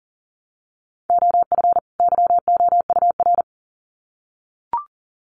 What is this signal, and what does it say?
Morse code, heard as a keyed tone, about 40 wpm, 700 Hz. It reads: OF YOUR
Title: Sets of 2 Words 40wpm